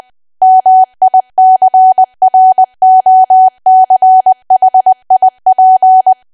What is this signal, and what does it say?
Morse code, heard as a keyed tone, about 20 wpm, 750 Hz. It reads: MICROCHIP
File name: cwa / W1AW-Level3 / 20WPM